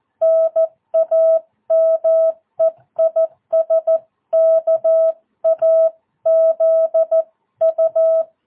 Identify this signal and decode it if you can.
Morse code, heard as a keyed tone, about 14 wpm, 650 Hz. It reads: NAME IS KAZU
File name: jh8cbh / 2022cw